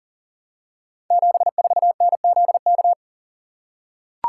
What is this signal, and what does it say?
Morse code, heard as a keyed tone, about 42 wpm, 700 Hz. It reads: 74NZK
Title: Code Groups - 5 Letter:Numbers - 40wpm